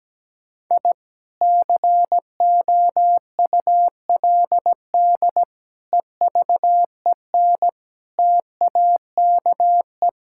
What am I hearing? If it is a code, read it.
Morse code, heard as a keyed tone, about 17 wpm, 700 Hz. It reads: I COULD EVEN TAKE
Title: Sentences from Top 1000 Words - Review Part 5 - 17wpm